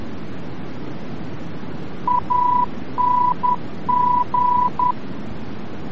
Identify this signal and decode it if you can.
Morse code, heard as a keyed tone, about 10 wpm, 990 Hz, with noise. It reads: ANG
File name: france / mx / ANG-VOR-F